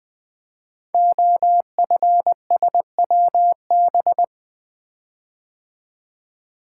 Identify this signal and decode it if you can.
Morse code, heard as a keyed tone, about 20 wpm, 700 Hz. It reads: OFSWB
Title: Code Groups - 5 Letters - 20wpm